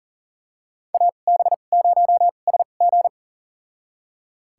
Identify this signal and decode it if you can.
Morse code, heard as a keyed tone, about 40 wpm, spaced as for 29 wpm, 700 Hz. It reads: AB0SG
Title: USA Call Signs - 5 Characters - 2x Character Spacing - 40wpm